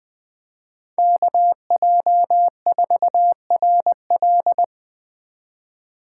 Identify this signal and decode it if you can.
Morse code, heard as a keyed tone, about 21 wpm, 700 Hz. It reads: KJ4RL